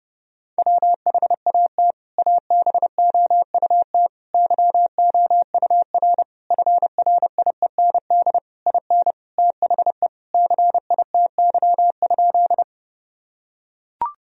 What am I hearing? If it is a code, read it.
Morse code, heard as a keyed tone, about 32 wpm, 700 Hz. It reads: WHAT ABOUT YOUR FRIEND IN THE CITY?
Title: Sentences from Top 1000 Words - Review Part 1 - 30wpm